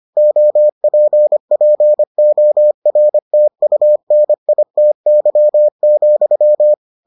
Morse code, heard as a keyed tone, about 25 wpm, 600 Hz. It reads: OPPORTUNITY,